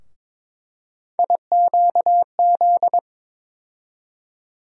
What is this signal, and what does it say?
Morse code, heard as a keyed tone, about 22 wpm, 700 Hz. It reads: IQZ
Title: USA Call Sign Postfixes - 3 Letters 22wpm